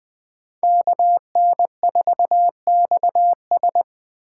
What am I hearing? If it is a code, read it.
Morse code, heard as a keyed tone, about 20 wpm, 700 Hz. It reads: KN4XS
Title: USA Call Signs - 5 Characters - 2x Character Spacing - 20wpm